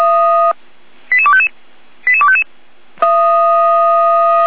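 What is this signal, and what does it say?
Five-tone selective call (ZVEI), heard as quick runs of five tones, tones of about 70 ms. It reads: 88255 88255
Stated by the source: ZVEI five-tone call received on SDR.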